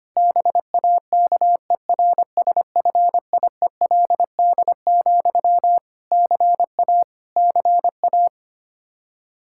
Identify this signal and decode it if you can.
Morse code, heard as a keyed tone, about 25 wpm, 700 Hz. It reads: BAKERSFIELD, CA CA
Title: Practice QSO - Exchange 1 - Beginning - 25wpm